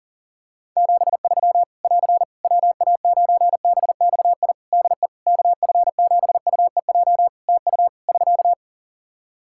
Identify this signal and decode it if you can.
Morse code, heard as a keyed tone, about 37 wpm, 700 Hz. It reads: 73 <AR> WA9BXI DE KF7UEJ TU <SK>
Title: K1USN SST QSO Practice 40wpm